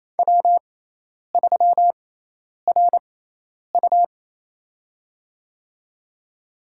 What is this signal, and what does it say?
Morse code, heard as a keyed tone, about 28 wpm, spaced as for 10 wpm, 700 Hz. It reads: W3RU
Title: USA Call Signs - 4 Characters - 6x Character Spacing - 28wpm